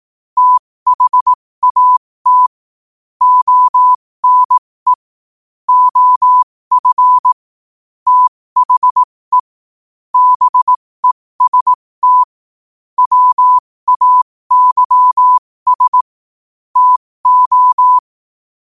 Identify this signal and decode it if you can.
Morse code, heard as a keyed tone, about 18 wpm, spaced as for 15 wpm, 1000 Hz. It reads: THAT ONE OF THE BEST WAYS TO
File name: qotdfiles / 15wpm